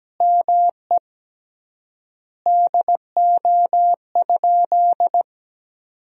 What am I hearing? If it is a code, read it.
Morse code, heard as a keyed tone, about 17 wpm, 700 Hz. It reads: ME DO?
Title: 3x Word-Spacing - Sentences from Top 200 Words - 17wpm